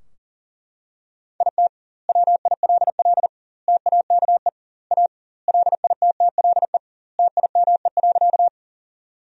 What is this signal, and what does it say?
Morse code, heard as a keyed tone, about 40 wpm, 700 Hz. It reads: IT WILL TAKE A LITTLE TIME.
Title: 2x Word-Spacing - Sentences from Top 200 Words - 40wpm